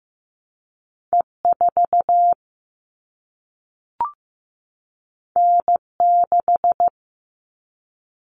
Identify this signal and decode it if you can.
Morse code, heard as a keyed tone, about 15 wpm, 700 Hz. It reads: E4 N6
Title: All Cut Numbers - 2-Digit 15wpm